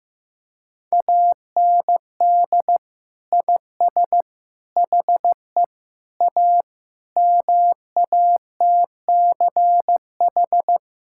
Morse code, heard as a keyed tone, about 15 wpm, 700 Hz. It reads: AND IS HE A MATCH
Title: Sentences from Top 1600 Words 15wpm